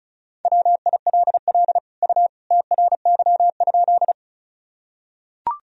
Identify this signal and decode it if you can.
Morse code, heard as a keyed tone, about 36 wpm, 700 Hz. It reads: WILL U TRY?
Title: Sentences from Top 200 Words and Abbreviations 35wpm